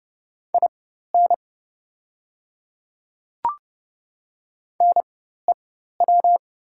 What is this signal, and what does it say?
Morse code, heard as a keyed tone, about 30 wpm, spaced as for 14 wpm, 700 Hz. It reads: IN NEW
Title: 4x Character Spacing - Top 300 Words - 30wpm